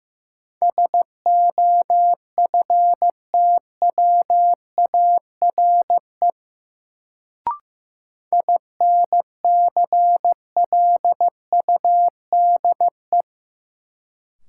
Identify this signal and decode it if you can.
Morse code, heard as a keyed tone, about 15 wpm, 700 Hz. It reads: SOFTWARE INCLUDE
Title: Sentences from Top 1000 Words - Review Part 3 - New Word Drill - 15wpm